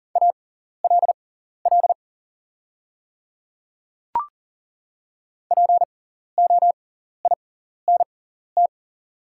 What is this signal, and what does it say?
Morse code, heard as a keyed tone, about 38 wpm, spaced as for 14 wpm, 700 Hz. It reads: ALL POINT
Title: 6x Character Spacing - Top 300 Words - 40wpm